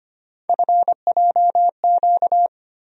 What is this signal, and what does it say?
Morse code, heard as a keyed tone, about 25 wpm, 700 Hz. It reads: FJQ
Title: USA Call Sign Postfixes - 3 Letters 25wpm